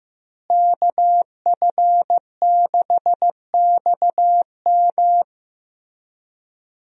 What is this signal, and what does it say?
Morse code, heard as a keyed tone, about 15 wpm, 700 Hz. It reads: KF6XM
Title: USA Call Signs - 5 Characters - Repeated 3x - 15wpm